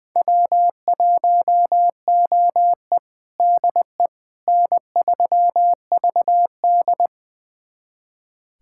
Morse code, heard as a keyed tone, about 21 wpm, 700 Hz. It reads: W1OE DE N3VD
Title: USA Callsign DE Callsign - 4 Characters - 20wpm